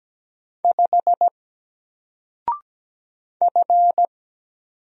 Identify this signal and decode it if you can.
Morse code, heard as a keyed tone, about 17 wpm, 700 Hz. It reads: 5 F
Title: Single Character (Letter:Number:Punctuation:Prosign) 17wpm